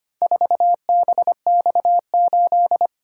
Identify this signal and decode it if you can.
Morse code, heard as a keyed tone, about 25 wpm, 700 Hz. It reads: 4BX8
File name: practice / 25wpm / Four Letter-Number Combinations - 25wpm